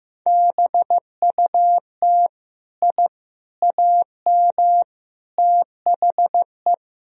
Morse code, heard as a keyed tone, about 16 wpm, 700 Hz. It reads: BUT I AM THE
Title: Sentences from Top 1000 Words - Review Part 4 - 15wpm